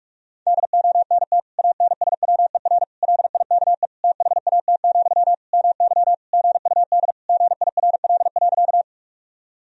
Morse code, heard as a keyed tone, about 42 wpm, 700 Hz. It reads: DONT ANSWER LIKE THAT, MY GUD GIRL.
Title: Sentences from Top 400 Words and Abbreviations 45wpm